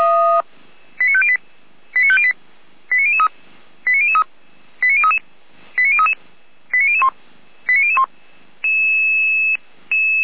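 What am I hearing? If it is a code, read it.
Five-tone selective call (ZVEI), heard as quick runs of five tones, tones of about 70 ms. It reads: 89508 89508 89003 89003 89033 89033 89001 89001
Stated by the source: ZVEI five-tone selective call received by SDR, archived by Feuerwehr Orlamünde fire brigade.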